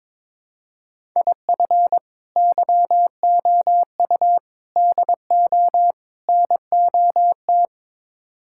Morse code, heard as a keyed tone, about 22 wpm, 700 Hz. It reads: IF YOU DO NOT
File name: practice / 22wpm / Build A Phrase - 5-Word Phrases - 22wpm